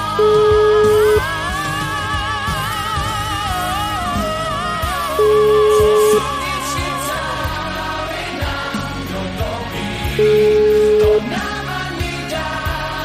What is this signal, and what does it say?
Ringback tone (Europe, 425 Hz), with noise.